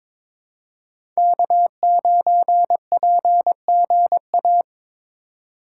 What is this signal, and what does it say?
Morse code, heard as a keyed tone, about 22 wpm, 700 Hz. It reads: K9PGA
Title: USA Call Signs - 5 Characters - 4x Character Spacing - 22wpm